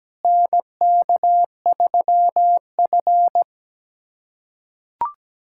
Morse code, heard as a keyed tone, about 17 wpm, 700 Hz. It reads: NK3F
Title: USA Callsign DE Callsign - 4 Characters - 17wpm